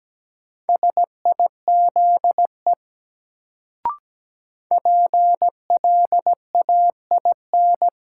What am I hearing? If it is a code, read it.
Morse code, heard as a keyed tone, about 17 wpm, 700 Hz. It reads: SIZE PLAIN
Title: Sentences from Top 400 Words - New Word Drill - 17wpm